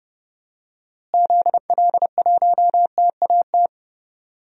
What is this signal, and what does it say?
Morse code, heard as a keyed tone, about 30 wpm, 700 Hz. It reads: ZL1TAT
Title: DXCC-100 Call Signs - 6 Characters - Repeated 3x - 30wpm